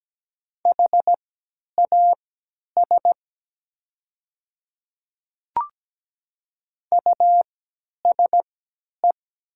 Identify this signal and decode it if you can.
Morse code, heard as a keyed tone, about 17 wpm, spaced as for 10 wpm, 700 Hz. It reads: HAS USE
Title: 3x Character Spacing - Top 100 Words - 17wpm